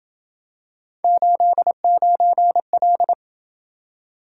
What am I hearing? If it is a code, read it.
Morse code, heard as a keyed tone, about 27 wpm, 700 Hz. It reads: 89L